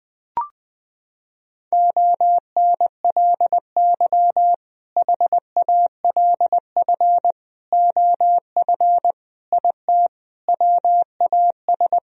Morse code, heard as a keyed tone, about 20 wpm, 700 Hz. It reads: ONLY HALF OF IT WAS
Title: Sentences from Top 1000 Words - Review Part 2 - 20wpm